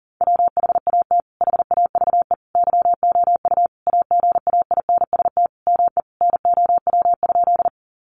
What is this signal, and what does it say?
Morse code, heard as a keyed tone, about 40 wpm, 700 Hz. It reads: WHAT HAVE YOU AGAINST ME NOW?
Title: Sentences from Top 400 Words 40wpm